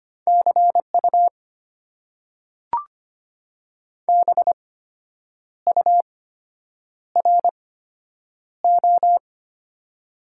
Morse code, heard as a keyed tone, about 25 wpm, spaced as for 8 wpm, 700 Hz. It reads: CU BURO
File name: Top 90 QSO Elements - 8x Character Spacing - 25wpm